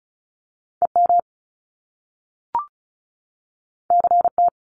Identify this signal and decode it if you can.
Morse code, heard as a keyed tone, about 36 wpm, 700 Hz. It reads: EM CT
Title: Top Two Letter Combinations in Words 35wpm